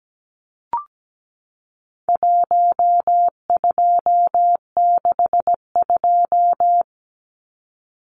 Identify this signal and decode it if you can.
Morse code, heard as a keyed tone, about 17 wpm, 700 Hz. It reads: 1262
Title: Numbers - 4-Digit 17wpm